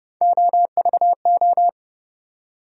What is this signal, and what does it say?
Morse code, heard as a keyed tone, about 30 wpm, 700 Hz. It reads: OVO
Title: USA Call Sign Postfixes - 3 Letters 30wpm